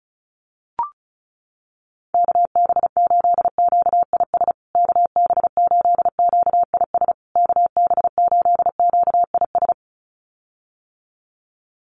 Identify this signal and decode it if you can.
Morse code, heard as a keyed tone, about 34 wpm, 700 Hz. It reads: KB8QIS KB8QIS KB8QIS
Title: USA Call Signs Repeated 3x 35wpm